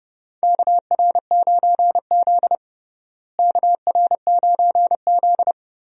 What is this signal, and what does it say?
Morse code, heard as a keyed tone, about 30 wpm, 700 Hz. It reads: KR9Z KR9Z
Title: USA Call Signs - 4 Characters - Repeated 3x - 30wpm